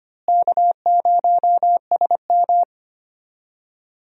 Morse code, heard as a keyed tone, about 25 wpm, 700 Hz. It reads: K0SM
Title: USA Call Signs - 4 Characters - Repeated 2x - 25wpm